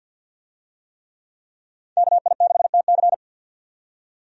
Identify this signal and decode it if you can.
Morse code, heard as a keyed tone, about 51 wpm, 700 Hz. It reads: KI6TC
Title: USA Call Signs - 5 Characters - 50wpm